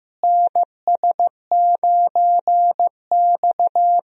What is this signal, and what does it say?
Morse code, heard as a keyed tone, about 15 wpm, 700 Hz. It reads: NS9X
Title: USA Call Signs - 4 Characters - Repeated 2x - 15wpm